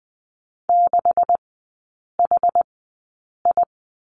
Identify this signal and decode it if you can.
Morse code, heard as a keyed tone, about 19 wpm, spaced as for 8 wpm, 700 Hz. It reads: 6HI
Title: 191 Lesson 31e - Letter v - Three Characters 20wpm